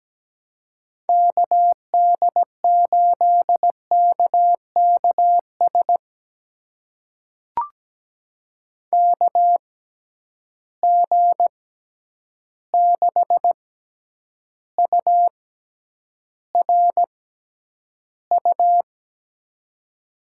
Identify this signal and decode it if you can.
Morse code, heard as a keyed tone, about 17 wpm, spaced as for 9 wpm, 700 Hz. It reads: KD8KKS KG6URU
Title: USA Call Signs - 6 Characters - 6x Character Spacing - 17wpm